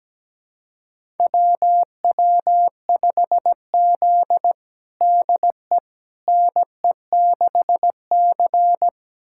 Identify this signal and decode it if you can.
Morse code, heard as a keyed tone, about 17 wpm, 700 Hz. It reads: WW5Z DE NE6C
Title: USA Callsign DE Callsign - 4 Characters - 17wpm